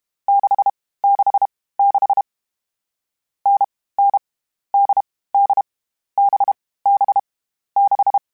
Morse code, heard as a keyed tone, about 32 wpm, 800 Hz. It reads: 6 6 6 N N D D B B 6